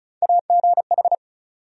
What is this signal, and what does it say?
Morse code, heard as a keyed tone, about 36 wpm, 700 Hz. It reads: AGH
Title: USA Call Sign Postfixes - 3 Letters 35wpm